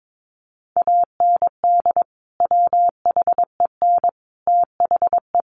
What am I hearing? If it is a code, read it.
Morse code, heard as a keyed tone, about 22 wpm, 700 Hz. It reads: AND WHEN THE
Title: Sentences from Top 1100 Words 22wpm